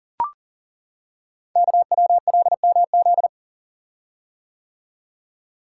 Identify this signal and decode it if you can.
Morse code, heard as a keyed tone, about 40 wpm, 690 Hz. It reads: KWLMZ
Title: Code Groups - 5 Letter:Numbers - 40wpm